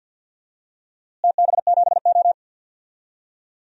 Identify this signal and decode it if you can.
Morse code, heard as a keyed tone, about 47 wpm, 700 Hz. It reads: TB7O
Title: DXCC-100 Call Signs - 4 Characters - Repeated 2x - 50wpm